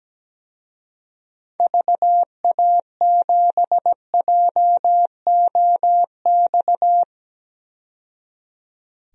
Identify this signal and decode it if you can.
Morse code, heard as a keyed tone, about 17 wpm, 700 Hz. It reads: VA7JOX
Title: DXCC-100 Call Signs - 6 Characters - 17wpm